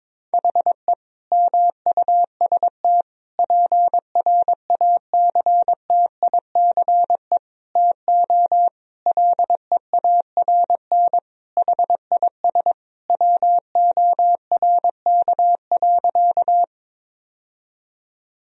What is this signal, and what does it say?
Morse code, heard as a keyed tone, about 21 wpm, 700 Hz. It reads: HE MUST PRACTICE TO LEARN HIS WORK.
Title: Sentences from Top 1000 Words - Review Part 4 - 22wpm